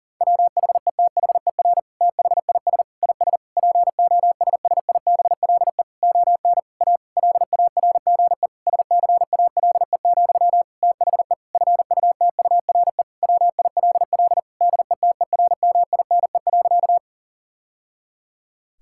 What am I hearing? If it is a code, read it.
Morse code, heard as a keyed tone, about 37 wpm, 700 Hz. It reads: WHETHER THIS IS POSSIBLE ON A LARGE SCALE, THE FUTURE WILL DETERMINE.
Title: Sentences from Top 1000 Words - Review Part 5 - 40wpm